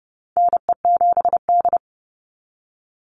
Morse code, heard as a keyed tone, about 30 wpm, 700 Hz. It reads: NE7D
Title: Call Signs - 30wpm